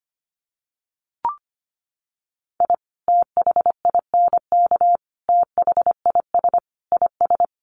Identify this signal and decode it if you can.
Morse code, heard as a keyed tone, about 25 wpm, 700 Hz. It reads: I THINK THIS IS
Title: Sets of 4 Words 25wpm